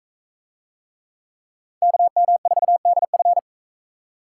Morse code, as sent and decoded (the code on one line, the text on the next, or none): -.- -- ....- -.. ..-.
KM4DF